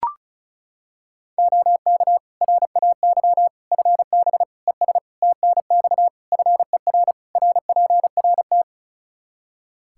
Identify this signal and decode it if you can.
Morse code, heard as a keyed tone, about 36 wpm, 700 Hz. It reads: OK RAY FB ES TNX FER RPRT
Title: Practice QSO - Exchange 2 - Info - 35wpm